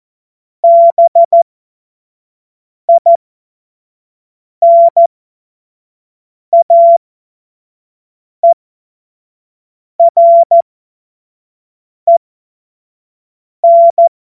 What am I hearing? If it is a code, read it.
Morse code, heard as a keyed tone, about 14 wpm, spaced as for 5 wpm, 680 Hz. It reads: BINAEREN
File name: Morsekurs / Witze / Witz 14. Dezember 2021, 25 F70